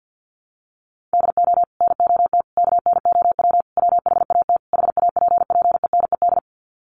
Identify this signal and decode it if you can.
Morse code, heard as a keyed tone, about 50 wpm, 700 Hz. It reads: DO NOT KNOW WHAT HAPPENED
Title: Sets of 5 Words Encore2 50wpm